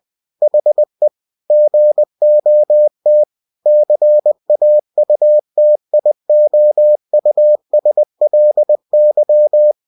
Morse code, heard as a keyed tone, about 20 wpm, 600 Hz. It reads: HE GOT CAUTIOUSLY
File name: cw-books / winnie-the-pooh / winnie-20wpm-page-0053